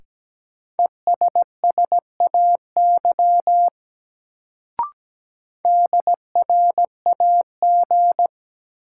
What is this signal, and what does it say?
Morse code, heard as a keyed tone, about 17 wpm, 700 Hz. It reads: ESSAY DRAG
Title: Sentences from Top 1500 Words - New Word Drill Alt - 17wpm